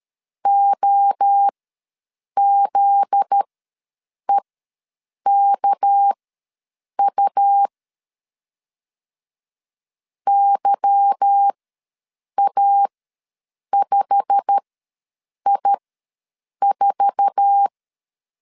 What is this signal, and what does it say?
Morse code, heard as a keyed tone, about 13 wpm, spaced as for 7 wpm, 800 Hz. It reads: OZEKU YA5I4
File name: LEZIONI / 8 YJ45 / 8 YJ45 ws 12wpm 3146